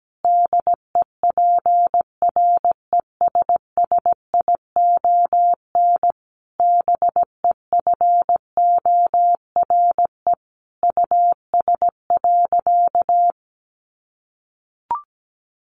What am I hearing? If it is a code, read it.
Morse code, heard as a keyed tone, about 17 wpm, 700 Hz. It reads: DEPRESSION BEFORE US.
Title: Sentences from Top 1300 Words 17wpm